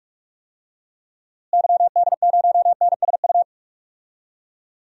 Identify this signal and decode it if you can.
Morse code, heard as a keyed tone, about 46 wpm, 700 Hz. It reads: YD0NSU
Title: DXCC-100 Call Signs - 6 Characters - Repeated 3x - 45wpm